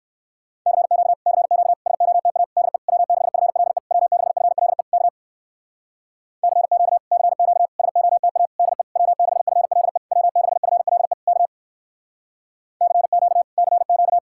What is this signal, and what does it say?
Morse code, heard as a keyed tone, about 58 wpm, 700 Hz. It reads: CQ CQ SOTA DE W6FLE W6FLE K CQ CQ SOTA DE W6FLE W6FLE K CQ CQ